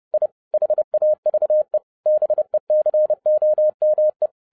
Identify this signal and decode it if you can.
Morse code, heard as a keyed tone, about 30 wpm, 610 Hz. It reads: I HAVE BECOME
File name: 2011-07-mp3 / n40 EXCITING QRP CW OPERATION 30wpm